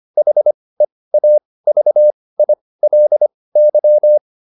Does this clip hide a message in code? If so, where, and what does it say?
Yes: Morse code from 0.2 to 4.2 s, heard as a keyed tone, about 25 wpm, spaced as for 18 wpm, 600 Hz. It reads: HEAVILY